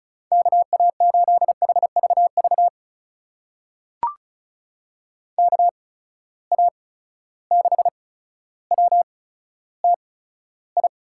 Morse code, heard as a keyed tone, about 34 wpm, spaced as for 15 wpm, 700 Hz. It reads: KA8HVV KA6WTI
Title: USA Call Signs - 6 Characters - 8x Character Spacing - 35wpm